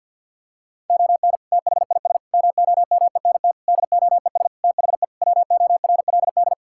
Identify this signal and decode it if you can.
Morse code, heard as a keyed tone, about 50 wpm, 700 Hz. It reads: ON THIS MOMENT DOES THE WORLD
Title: Sentences from Top 1600 Words 50wpm